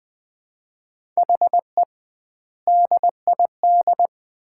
Morse code, heard as a keyed tone, about 20 wpm, 700 Hz. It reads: HE DID
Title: 2x Word-Spacing - Sets of 3 Words - 20wpm